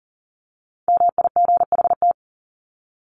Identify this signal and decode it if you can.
Morse code, heard as a keyed tone, about 40 wpm, 700 Hz. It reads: MIGHT
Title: Top 500 Words 40wpm